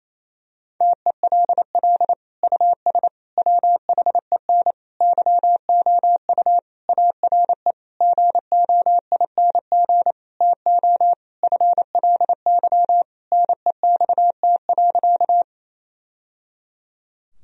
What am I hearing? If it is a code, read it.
Morse code, heard as a keyed tone, about 27 wpm, 700 Hz. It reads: TELL US WHEN YOU ARE GOING TO FLY NEXT.
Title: Sentences from Top 1000 Words - Review Part 2 - 28wpm